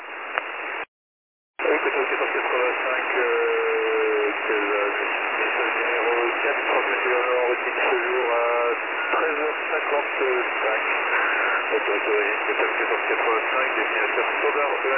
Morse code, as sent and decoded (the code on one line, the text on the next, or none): none